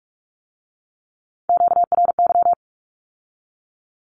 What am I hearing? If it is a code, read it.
Morse code, heard as a keyed tone, about 46 wpm, 700 Hz. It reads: QRY
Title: Q-Codes with Spoken Meaning - 45wpm